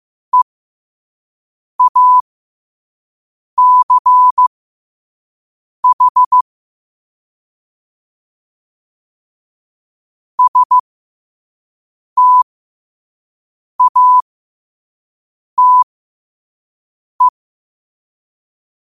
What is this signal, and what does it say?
Morse code, heard as a keyed tone, about 15 wpm, spaced as for 5 wpm, 1000 Hz. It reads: EACH STATE
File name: qotdfiles / 5wpm